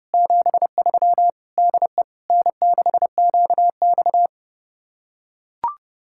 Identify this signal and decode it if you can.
Morse code, heard as a keyed tone, about 30 wpm, 700 Hz. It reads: 73 DE N6QX